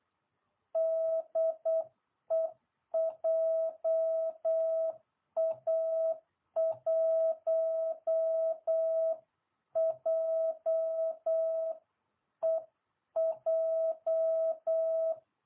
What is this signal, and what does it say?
Morse code, heard as a keyed tone, about 8 wpm, 660 Hz. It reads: DE JA1JEJ